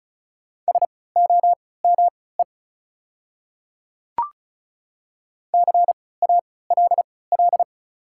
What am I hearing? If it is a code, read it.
Morse code, heard as a keyed tone, about 34 wpm, spaced as for 20 wpm, 700 Hz. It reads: SOME CALL
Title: 3x Character Spacing - Top 200 Words - 35wpm